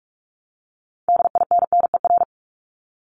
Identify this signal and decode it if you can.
Morse code, heard as a keyed tone, about 46 wpm, 700 Hz. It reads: DINNER
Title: Multi-letter Phonemes Repeated 3x 45wpm